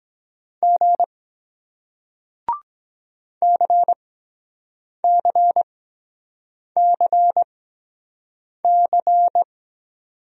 Morse code, heard as a keyed tone, about 23 wpm, spaced as for 7 wpm, 700 Hz. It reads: G CCCC